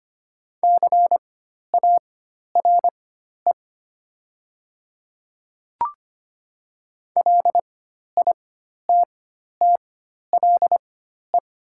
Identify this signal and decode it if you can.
Morse code, heard as a keyed tone, about 25 wpm, spaced as for 12 wpm, 700 Hz. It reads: CARE LITTLE
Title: 4x Character Spacing - Top 300 Words - 25wpm